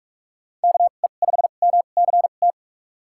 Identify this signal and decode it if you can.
Morse code, heard as a keyed tone, about 46 wpm, spaced as for 32 wpm, 700 Hz. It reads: KE5MCT
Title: USA Call Signs - 6 Characters - 2x Character Spacing - 45wpm